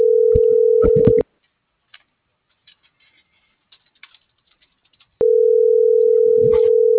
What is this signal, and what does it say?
Ringback tone (North America), 440 + 480 Hz.